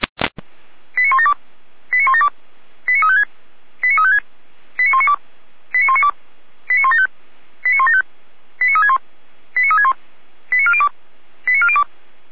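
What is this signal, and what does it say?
Five-tone selective call (ZVEI), heard as quick runs of five tones, tones of about 80 ms. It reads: 89172 89172 89356 89356 89192 89192 89175 89175 89361 89361 89402 89402
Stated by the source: ZVEI five-tone selective call received on SDR, Feuerwehr Orlamünde.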